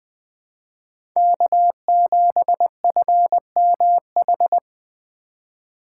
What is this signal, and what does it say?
Morse code, heard as a keyed tone, about 20 wpm, 700 Hz. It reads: K7FMH
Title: USA Call Signs - 3x Character Spacing - 20wpm